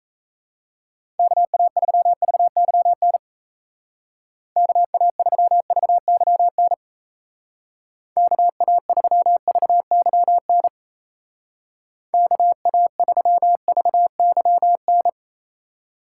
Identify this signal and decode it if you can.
Morse code, heard as a keyed tone, about 37 wpm, 700 Hz. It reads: KA3VYN KA3VYN KA3VYN KA3VYN